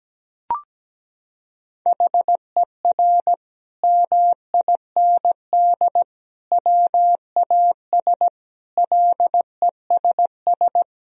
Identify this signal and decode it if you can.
Morse code, heard as a keyed tone, about 17 wpm, 700 Hz. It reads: HER MIND WAS LESS
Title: Sentences from Top 1000 Words - Review Part 2 - 17wpm